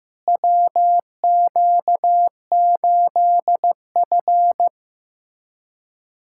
Morse code, heard as a keyed tone, about 15 wpm, 700 Hz. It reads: WQ8F
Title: USA Callsign DE Callsign - 4 Characters - 15wpm